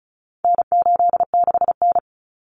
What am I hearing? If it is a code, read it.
Morse code, heard as a keyed tone, about 34 wpm, 700 Hz. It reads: N86N
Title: Common Cut Numbers - 4-Digit 35wpm